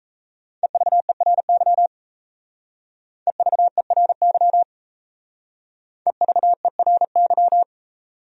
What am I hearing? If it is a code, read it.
Morse code, heard as a keyed tone, about 35 wpm, 700 Hz. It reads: EVERY EVERY EVERY